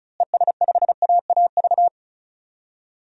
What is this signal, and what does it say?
Morse code, heard as a keyed tone, about 36 wpm, 700 Hz. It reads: ES5AAV
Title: DXCC-100 Call Signs - 6 Characters - Repeated 3x - 35wpm